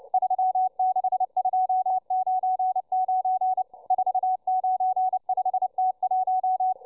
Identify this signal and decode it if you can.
Morse code, heard as a keyed tone, about 29 wpm, 750 Hz, with radio noise. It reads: 36299 495T1